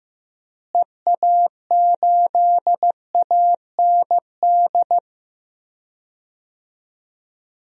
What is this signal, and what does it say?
Morse code, heard as a keyed tone, about 15 wpm, 700 Hz. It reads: EA8AND
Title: DXCC-100 Call Signs - 6 Characters - Repeated 2x - 15wpm